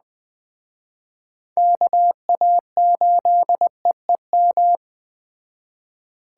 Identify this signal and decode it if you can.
Morse code, heard as a keyed tone, about 20 wpm, 700 Hz. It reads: KA8EEM